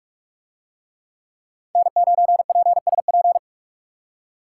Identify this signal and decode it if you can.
Morse code, heard as a keyed tone, about 44 wpm, 700 Hz. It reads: N9PSP